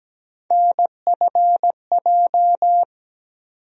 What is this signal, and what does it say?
Morse code, heard as a keyed tone, about 17 wpm, 700 Hz. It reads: NFJ
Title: USA Call Sign Postfixes - 3 Letters 17wpm